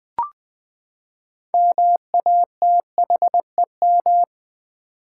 Morse code, heard as a keyed tone, about 20 wpm, 700 Hz. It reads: MATHEM
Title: Build A Word - 6 Letters At A Time - 12-Letter Words - 20wpm